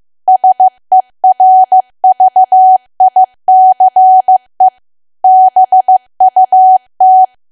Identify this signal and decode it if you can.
Morse code, heard as a keyed tone, about 15 wpm, 750 Hz. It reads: SERVICE BUT
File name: cwa / W1AW-Level2 / 15WPM